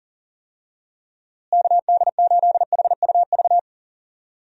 Morse code, heard as a keyed tone, about 40 wpm, 700 Hz. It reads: KD8HUV
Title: USA Call Signs - 6 Characters - Repeated 2x - 40wpm